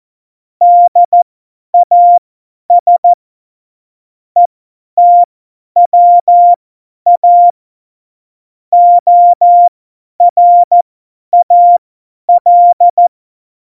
Morse code, heard as a keyed tone, about 14 wpm, spaced as for 10 wpm, 700 Hz. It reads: DAS ETWA ORAL
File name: Morsekurs / Witze / Witz 13. Oktober 2020, 50 F70